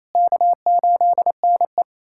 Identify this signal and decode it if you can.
Morse code, heard as a keyed tone, about 27 wpm, 690 Hz. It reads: K8NE
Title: SOTA QSO Practice 28wpm